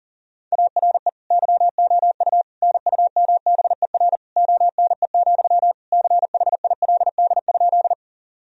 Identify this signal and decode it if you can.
Morse code, heard as a keyed tone, about 40 wpm, 700 Hz. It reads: ARE YOU NUMBER ONE, CHILD?
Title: Sentences from Top 1000 Words - Review Part 1 - 40wpm